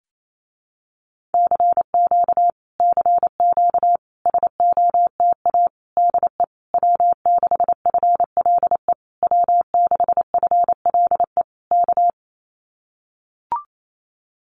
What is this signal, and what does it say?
Morse code, heard as a keyed tone, about 28 wpm, 700 Hz. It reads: CQ CQ SOTA DE W6FLE W6FLE K
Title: Common QSO Phrases Encore 28wpm